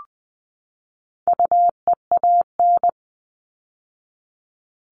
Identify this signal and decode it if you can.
Morse code, heard as a keyed tone, about 20 wpm, 700 Hz. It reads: UEAN
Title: All Cut Numbers - 4-Digit 20wpm